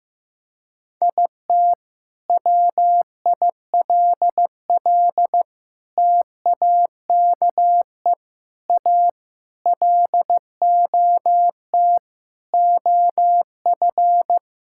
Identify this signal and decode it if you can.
Morse code, heard as a keyed tone, about 15 wpm, 700 Hz. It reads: IT WILL TAKE A LOT OF
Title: Sentences from Top 1000 Words - Review Part 6 - 15wpm